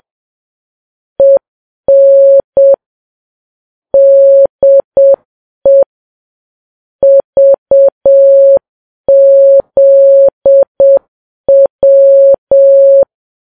Morse code, heard as a keyed tone, about 7 wpm, 550 Hz. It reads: EN DE VZW
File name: Files Web tls / Morse on9fv / morse7kps